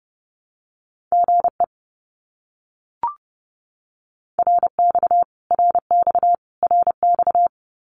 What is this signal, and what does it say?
Morse code, heard as a keyed tone, about 30 wpm, 700 Hz. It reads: GE RX RX RX
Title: Abbreviations Repeated 3x - 30wpm